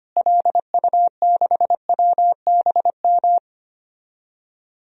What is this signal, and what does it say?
Morse code, heard as a keyed tone, about 25 wpm, 700 Hz. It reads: LU6WBM
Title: DXCC-100 Call Signs - 6 Characters - Repeated 2x - 25wpm